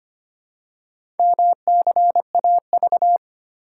Morse code, heard as a keyed tone, about 25 wpm, 700 Hz. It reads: MCAV
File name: practice / 25wpm / Four Letter-Number Combinations - 25wpm